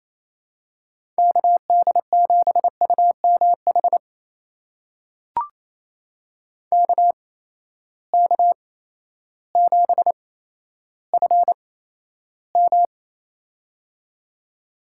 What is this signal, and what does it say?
Morse code, heard as a keyed tone, about 28 wpm, spaced as for 13 wpm, 700 Hz. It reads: KD7UMH KK7FM
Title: USA Call Signs - 8x Character Spacing - 28wpm